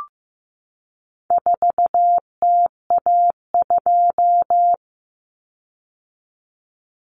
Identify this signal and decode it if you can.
Morse code, heard as a keyed tone, about 15 wpm, 700 Hz. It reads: 4TA2
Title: Common Cut Numbers - 4-Digit 15wpm